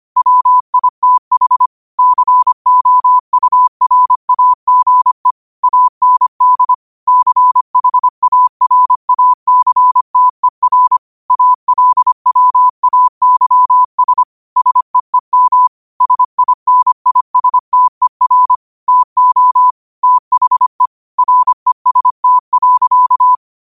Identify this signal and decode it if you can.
Morse code, heard as a keyed tone, about 25 wpm, 1000 Hz. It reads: WITH COURAGE AND CHARACTER ALWAYS SEEM SINISTER TO THE REST.
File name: qotdfiles / 25wpm